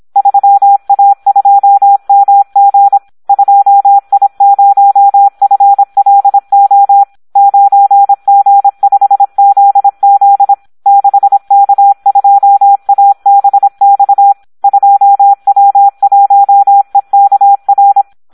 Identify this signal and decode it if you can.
Morse code, heard as a keyed tone, about 26 wpm, 800 Hz, with radio noise. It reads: 3A2MG 2I0FLO 9G5ZZ 6K2ABX 2W1EKR
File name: Audio practica CW / SD 1 26